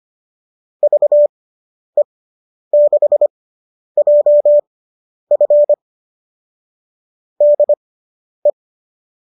Morse code, heard as a keyed tone, about 25 wpm, spaced as for 10 wpm, 590 Hz. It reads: VE6JF DE